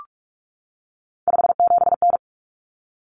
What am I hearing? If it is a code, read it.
Morse code, heard as a keyed tone, about 44 wpm, 700 Hz. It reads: 57N
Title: Common Cut Numbers - 3-Digit 45wpm